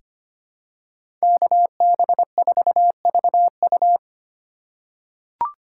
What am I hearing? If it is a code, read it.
Morse code, heard as a keyed tone, about 25 wpm, 700 Hz. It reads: KB4VU
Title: K1USN SST QSO Practice - 25wpm